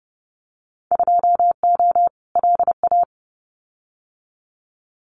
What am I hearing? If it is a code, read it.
Morse code, heard as a keyed tone, about 30 wpm, 700 Hz. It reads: 2O LA
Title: Winter Field Day Exchanges 30wpm